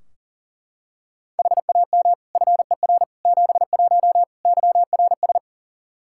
Morse code, heard as a keyed tone, about 37 wpm, 700 Hz. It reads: HAM FER 71 YRS
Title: Practice QSO - Exchange 3 - More Info - 40wpm